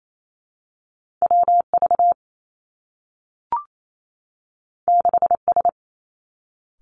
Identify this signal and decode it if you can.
Morse code, heard as a keyed tone, about 28 wpm, 700 Hz. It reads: W V 6 S
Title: Two Challenging Letter-Number Combinations 28wpm